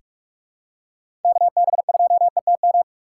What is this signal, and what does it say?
Morse code, heard as a keyed tone, about 42 wpm, 700 Hz. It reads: KB2ETM
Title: USA Call Signs - 6 Characters - 2x Character Spacing - 45wpm